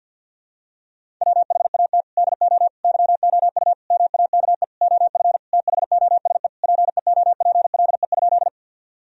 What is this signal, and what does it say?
Morse code, heard as a keyed tone, about 51 wpm, 700 Hz. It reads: WHAT DO YOU MAKE OF THOSE PEOPLE?
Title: Sentences from Top 1000 Words - Review Part 1 - 50wpm